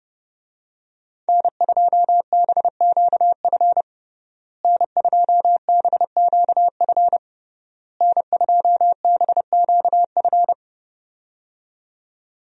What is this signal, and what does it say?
Morse code, heard as a keyed tone, about 30 wpm, 700 Hz. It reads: N2BQF N2BQF N2BQF
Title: USA Call Signs - 5 Characters - Repeated 3x - 30wpm